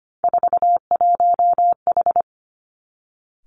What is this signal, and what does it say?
Morse code, heard as a keyed tone, about 25 wpm, 700 Hz. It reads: 41H
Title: Three Challenging Letter-Number Combinations 25wpm